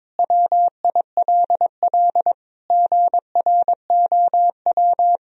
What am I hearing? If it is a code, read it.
Morse code, heard as a keyed tone, about 22 wpm, 700 Hz. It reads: WILL GROW
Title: Sentences from Top 1000 Words - Review Part 2 - 22wpm